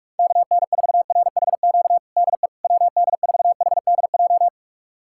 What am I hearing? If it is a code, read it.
Morse code, heard as a keyed tone, about 46 wpm, 700 Hz. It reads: KN4RHQ DE WD4HDJ
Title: USA Callsign DE Callsign - 6 Characters - 45wpm